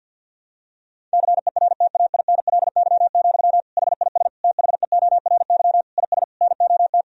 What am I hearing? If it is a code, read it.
Morse code, heard as a keyed tone, about 50 wpm, 700 Hz. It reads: CERTAINLY, HIS THEORY IS NOT
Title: Sentences from Top 1000 Words - Review Part 3 - 50wpm